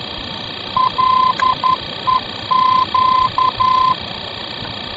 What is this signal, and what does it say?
Morse code, heard as a keyed tone, about 11 wpm, 1030 Hz, with noise. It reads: LEQ
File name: france / mx / LEQ-F-VOR